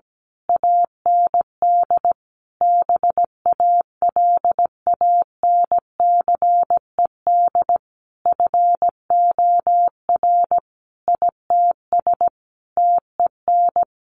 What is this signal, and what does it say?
Morse code, heard as a keyed tone, about 17 wpm, 700 Hz. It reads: AND BALANCED FOR ITS TEN
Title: A Sound Of Thunder 17wpm